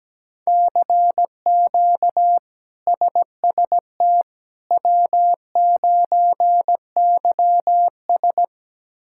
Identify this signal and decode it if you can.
Morse code, heard as a keyed tone, about 17 wpm, 700 Hz. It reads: CQ SST W9YS
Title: K1USN SST QSO Practice - 17wpm